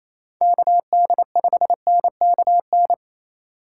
Morse code, heard as a keyed tone, about 27 wpm, 700 Hz. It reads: KD5NKN
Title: USA Call Signs - 3x Character Spacing - 28wpm